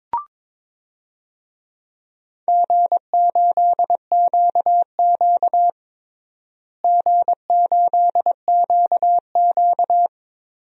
Morse code, heard as a keyed tone, about 22 wpm, 700 Hz. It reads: G8QQ G8QQ
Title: DXCC-100 Call Signs - 4 Characters - Repeated 2x - 22wpm